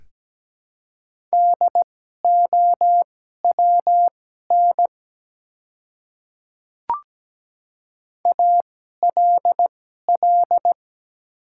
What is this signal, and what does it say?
Morse code, heard as a keyed tone, about 17 wpm, spaced as for 12 wpm, 700 Hz. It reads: DOWN ALL
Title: 2x Character Spacing - Top 100 Words - 17wpm